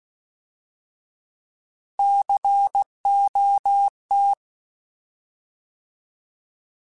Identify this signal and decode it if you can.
Morse code, heard as a keyed tone, about 16 wpm, 800 Hz. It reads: COT
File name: Alphabetical / Alphabetical mp3s / Cs @15